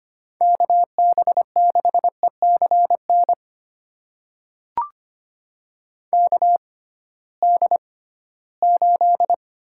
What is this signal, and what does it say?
Morse code, heard as a keyed tone, about 25 wpm, spaced as for 16 wpm, 700 Hz. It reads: KB6ECN KD8
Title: USA Call Signs - 6 Characters - 6x Character Spacing - 25wpm